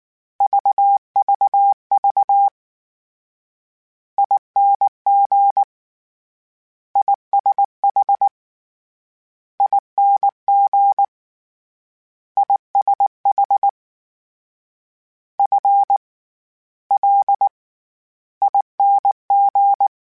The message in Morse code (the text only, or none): VVV ING ISH ING ISH F L ING